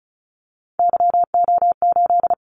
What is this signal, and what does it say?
Morse code, heard as a keyed tone, about 36 wpm, 700 Hz. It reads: YO8
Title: Callsign Prefixes for Top 25 DXCC - 35wpm